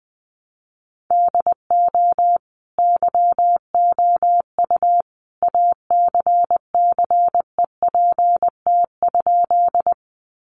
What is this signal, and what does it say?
Morse code, heard as a keyed tone, about 20 wpm, 700 Hz. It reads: DO YOU ACCEPT?
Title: Sentences from Top 1300 Words 20wpm